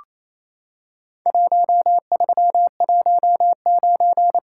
Morse code, heard as a keyed tone, about 28 wpm, 700 Hz. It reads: 1319
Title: Worked All Europe DX Contest - QTC Exchange - 28wpm